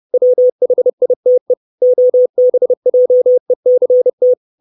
Morse code, heard as a keyed tone, about 30 wpm, 500 Hz. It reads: WHITE OBJECT
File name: 632 Words 30WPM